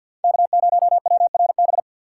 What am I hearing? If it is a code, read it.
Morse code, heard as a keyed tone, about 53 wpm, 700 Hz. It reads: K0WRB
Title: USA Call Signs - 4x Character Spacing - 50wpm